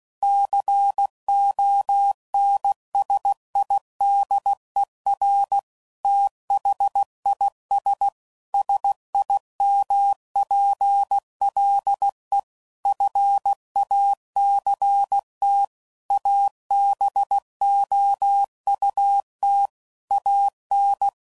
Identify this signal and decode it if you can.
Morse code, heard as a keyed tone, about 16 wpm, 800 Hz. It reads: CONSIDER THIS SIMPLE FACT ABOUT AN
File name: CW Code Groups @15WPM / Technical Subjects / OR Gate